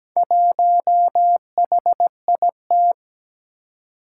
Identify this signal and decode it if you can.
Morse code, heard as a keyed tone, about 17 wpm, 700 Hz. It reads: 1HIT
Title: Four Challenging Letter-Number Combinations 17wpm